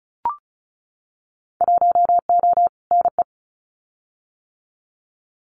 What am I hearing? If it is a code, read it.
Morse code, heard as a keyed tone, about 36 wpm, 700 Hz. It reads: 1O NE
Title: Winter Field Day Exchanges 35wpm